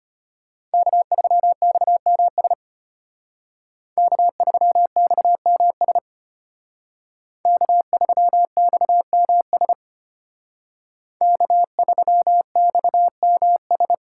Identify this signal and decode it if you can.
Morse code, heard as a keyed tone, about 32 wpm, 700 Hz. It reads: K3XMS K3XMS K3XMS K3XMS